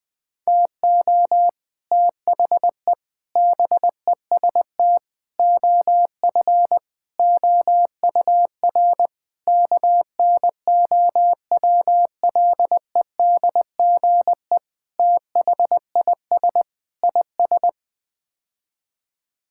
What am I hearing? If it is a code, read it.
Morse code, heard as a keyed tone, about 20 wpm, 700 Hz. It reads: TO THE BEST OF OUR KNOWLEDGE THIS IS
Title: Sets of 8 Words - 20wpm